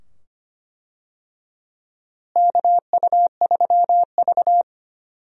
Morse code, heard as a keyed tone, about 25 wpm, 700 Hz. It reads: KU3V
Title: USA Call Signs - 4 Characters - Repeated 2x - 25wpm